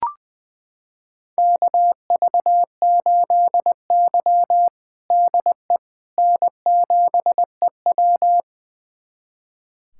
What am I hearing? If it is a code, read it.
Morse code, heard as a keyed tone, about 21 wpm, 700 Hz. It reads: KV8Y DE N7EW
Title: USA Callsign DE Callsign - 4 Characters - 20wpm